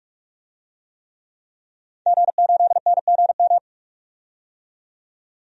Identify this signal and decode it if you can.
Morse code, heard as a keyed tone, about 47 wpm, 700 Hz. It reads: G8NGM
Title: DXCC-100 Call Signs - 5 Characters - 45wpm